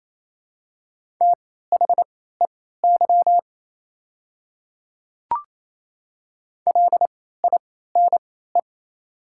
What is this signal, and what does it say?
Morse code, heard as a keyed tone, about 28 wpm, spaced as for 16 wpm, 700 Hz. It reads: THEY LINE
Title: 3x Character Spacing - Top 200 Words - 28wpm